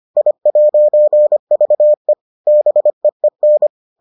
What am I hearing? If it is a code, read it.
Morse code, heard as a keyed tone, about 25 wpm, 600 Hz. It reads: I'VE BEEN